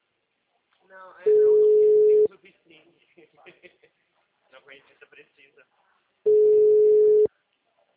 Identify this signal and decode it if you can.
Ringback tone (Europe, 425 Hz), with speech.